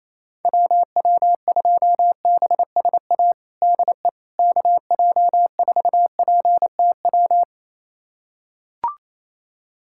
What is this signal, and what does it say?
Morse code, heard as a keyed tone, about 28 wpm, 700 Hz. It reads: WW2BSA DE KJ4PTW
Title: USA Callsign DE Callsign - 6 Characters - 28wpm